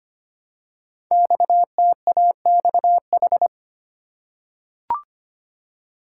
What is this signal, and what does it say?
Morse code, heard as a keyed tone, about 25 wpm, 700 Hz. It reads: XTAXH
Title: Code Groups - 5 Letters - 25wpm